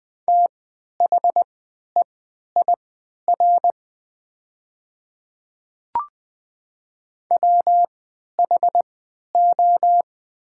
Morse code, heard as a keyed tone, about 20 wpm, spaced as for 11 wpm, 700 Hz. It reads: THEIR WHO
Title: 3x Character Spacing - Top 100 Words - 20wpm